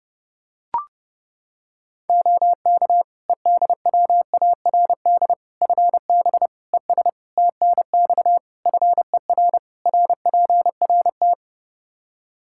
Morse code, heard as a keyed tone, about 30 wpm, 700 Hz. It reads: OK EDWARD FB ES TNX FER RPRT
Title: Practice QSO - Exchange 2 - Info - 30wpm